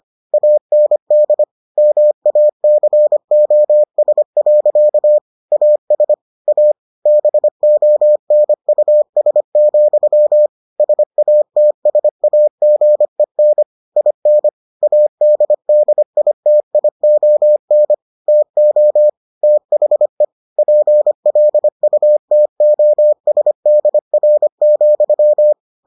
Morse code, heard as a keyed tone, about 25 wpm, 600 Hz. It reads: AND MACOS. AS A BONUS, SATSAGEN IN ADDITION TO THE PLUTOSDR,